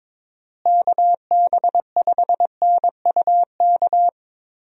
Morse code, heard as a keyed tone, about 22 wpm, 700 Hz. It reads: KB5NUK
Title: USA Call Signs - 6 Characters - Repeated 2x - 22wpm